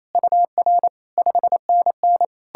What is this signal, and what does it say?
Morse code, heard as a keyed tone, about 37 wpm, 700 Hz. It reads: UR 5NN